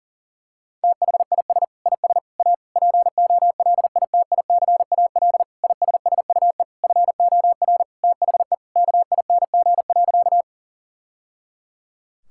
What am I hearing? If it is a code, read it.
Morse code, heard as a keyed tone, about 37 wpm, 700 Hz. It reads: THIS IS A POLITICAL ISSUE FOR THE KING.
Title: Sentences from Top 1000 Words - Review Part 6 - 40wpm